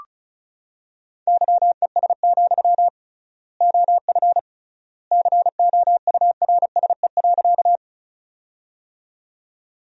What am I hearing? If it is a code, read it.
Morse code, heard as a keyed tone, about 36 wpm, 700 Hz. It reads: YES, OF COURSE.
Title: 3x Word-Spacing - Sentences from Top 500 Words - 35wpm